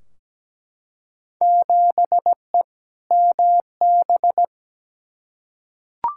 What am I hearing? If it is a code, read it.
Morse code, heard as a keyed tone, about 17 wpm, 700 Hz. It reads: 7E MB
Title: Field Day Exchanges Encore - 17wpm